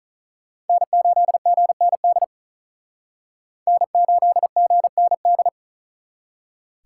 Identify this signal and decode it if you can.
Morse code, heard as a keyed tone, about 46 wpm, 700 Hz. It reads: N8GND N8GND